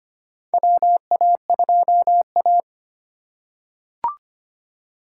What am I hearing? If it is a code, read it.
Morse code, heard as a keyed tone, about 25 wpm, 700 Hz. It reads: WA2A
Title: USA Call Signs - 3x Character Spacing - 25wpm